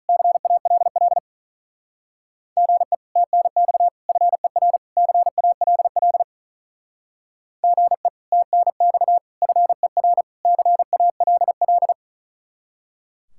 Morse code, heard as a keyed tone, about 46 wpm, 700 Hz. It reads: CALL GE TNX FER CALL GE TNX FER CALL